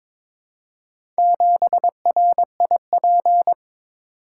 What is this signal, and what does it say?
Morse code, heard as a keyed tone, about 22 wpm, 700 Hz. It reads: 7RIP
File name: practice / 22wpm / Four Letter-Number Combinations - 22wpm